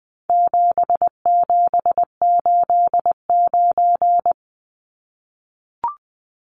Morse code, heard as a keyed tone, about 20 wpm, 700 Hz. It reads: 7789
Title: Numbers - 4-Digit 20wpm